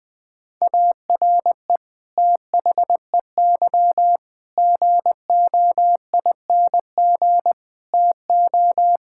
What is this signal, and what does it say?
Morse code, heard as a keyed tone, about 20 wpm, 700 Hz. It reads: ARE THEY GOING TO
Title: Sentences from Top 1600 Words 20wpm